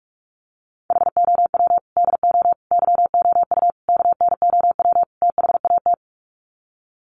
Morse code, heard as a keyed tone, about 44 wpm, 700 Hz. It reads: HOW DO YOU KNOW THAT
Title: Sets of 5 Words Encore 45wpm